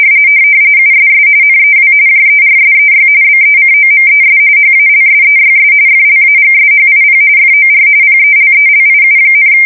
RTTY, about 45.45 baud. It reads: USS NEW JERSEY BECOMES HALSEYS FLAGSHIP FOR PELELIU,
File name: fsk170 / USS-New-Jersey